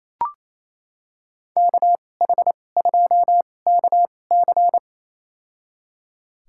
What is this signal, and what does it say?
Morse code, heard as a keyed tone, about 28 wpm, spaced as for 20 wpm, 700 Hz. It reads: KH2KC
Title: USA Call Signs - 5 Characters - 2x Character Spacing - 28wpm